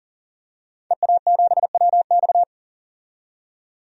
Morse code, heard as a keyed tone, about 40 wpm, 700 Hz. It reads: EA7WX
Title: DXCC-100 Call Signs - 5 Characters - 40wpm